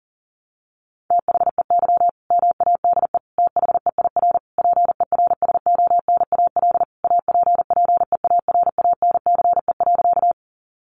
Morse code, heard as a keyed tone, about 40 wpm, 700 Hz. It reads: THEY MADE THEIR PERSONAL APPEARANCE.
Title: Sentences from Top 1100 Words 40wpm